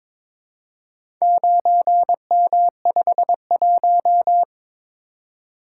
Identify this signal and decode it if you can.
Morse code, heard as a keyed tone, about 22 wpm, 700 Hz. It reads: 9M51
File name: practice / 22wpm / Four Letter-Number Combinations - 22wpm